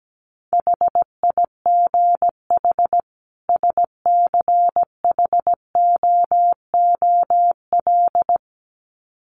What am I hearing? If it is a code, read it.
Morse code, heard as a keyed tone, about 17 wpm, 700 Hz. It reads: HIGH SCHOOL
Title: Sets of 2 Words 17wpm